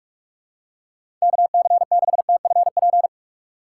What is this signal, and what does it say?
Morse code, heard as a keyed tone, about 42 wpm, 700 Hz. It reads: KC6TFP
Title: USA Call Signs - 6 Characters - 3x Character Spacing - 45wpm